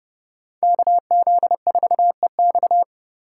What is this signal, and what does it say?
Morse code, heard as a keyed tone, about 36 wpm, 700 Hz. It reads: KZ4EX
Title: Speed Racing - USA 2x2 Call Signs 30wpm